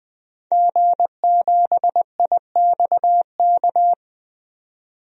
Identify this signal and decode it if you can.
Morse code, heard as a keyed tone, about 20 wpm, 700 Hz. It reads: G7IXK
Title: DXCC-100 Call Signs - 5 Characters - 20wpm